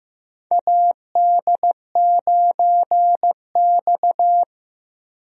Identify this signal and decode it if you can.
Morse code, heard as a keyed tone, about 15 wpm, 700 Hz. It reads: AD9X
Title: USA Call Signs - 4 Characters - Repeated 2x - 15wpm